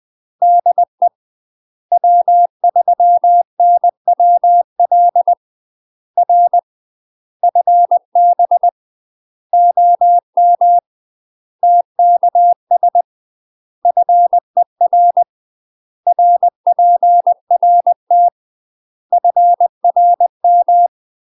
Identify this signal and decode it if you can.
Morse code, heard as a keyed tone, about 20 wpm, 700 Hz. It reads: DE W3NWL R FB OM TKS FER RPRT FRM
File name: UK MASTER QSOs TRAINING / 20WPM / UK MASTER QSOs 0026